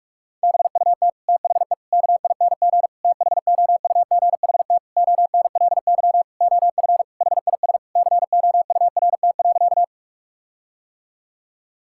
Morse code, heard as a keyed tone, about 46 wpm, 700 Hz. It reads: BUT THE KING THOUGHT ONLY OF HIS COURT.
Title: Sentences from Top 1600 Words 45wpm